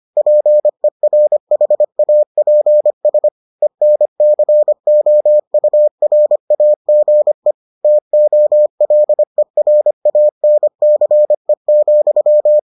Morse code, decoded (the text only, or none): PERHAPS ENCOURAGE TOLERANCE,